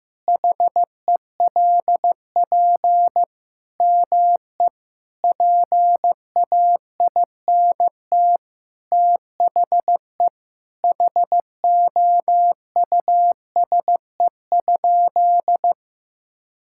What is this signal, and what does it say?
Morse code, heard as a keyed tone, about 15 wpm, 700 Hz. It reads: HELP ME PAINT THE HOUSE?
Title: Sentences from Top 1000 Words - Review Part 6 - 15wpm